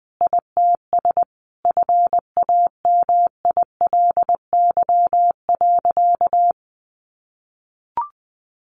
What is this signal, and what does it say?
Morse code, heard as a keyed tone, about 20 wpm, 700 Hz. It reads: ITS FAMILY.
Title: Additional Easy Phrases 20wpm